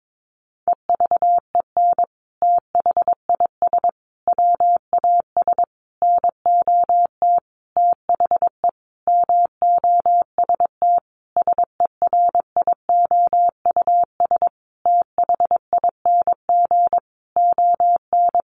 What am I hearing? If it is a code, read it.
Morse code, heard as a keyed tone, about 22 wpm, 700 Hz. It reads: EVEN THIS WAS NOT THE MOST SERIOUS THING ON
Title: Sentences from Top 1100 Words 22wpm